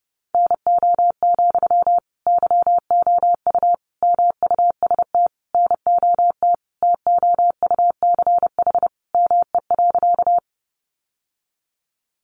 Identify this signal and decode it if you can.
Morse code, heard as a keyed tone, about 30 wpm, 700 Hz. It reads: NO, YOU MUST NOT TOUCH ME.
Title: Sentences from Top 1200 Words 30wpm